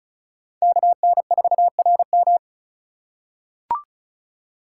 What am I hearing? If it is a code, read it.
Morse code, heard as a keyed tone, about 36 wpm, 700 Hz. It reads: KN4RM
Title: USA Call Signs - 5 Characters - 3x Character Spacing - 35wpm